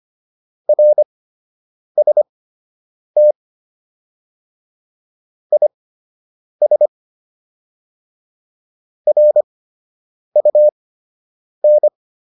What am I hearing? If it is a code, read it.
Morse code, heard as a keyed tone, about 25 wpm, spaced as for 8 wpm, 600 Hz. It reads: RST IS RUN